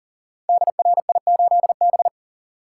Morse code, heard as a keyed tone, about 41 wpm, 700 Hz. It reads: DRI8B
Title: Code Groups - 5 Letter:Numbers - 40wpm